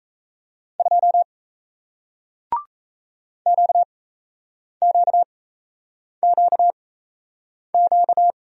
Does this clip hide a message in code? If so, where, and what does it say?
Yes: Morse code from 0.8 to 8.3 s, heard as a keyed tone, about 37 wpm, spaced as for 8 wpm, 700 Hz. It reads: 2 QQQQ